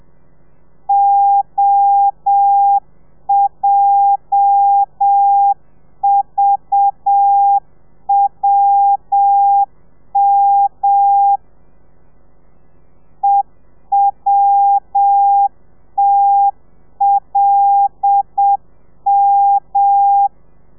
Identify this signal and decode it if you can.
Morse code, heard as a keyed tone, about 7 wpm, 800 Hz. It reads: OJVWM EWTLM
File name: telegrafia / i2tfj / Lez8 30